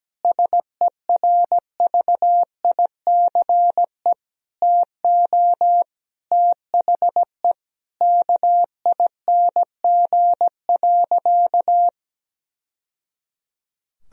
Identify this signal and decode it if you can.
Morse code, heard as a keyed tone, about 17 wpm, 700 Hz. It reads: SERVICE TO THE KING.
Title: Sentences from Top 1000 Words - Review Part 4 - 17wpm